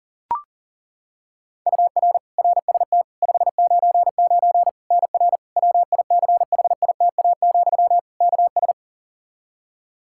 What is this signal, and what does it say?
Morse code, heard as a keyed tone, about 40 wpm, 700 Hz. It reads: UR RST 599 NR WICHITA, KS
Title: K1USN SST QSO Practice 40wpm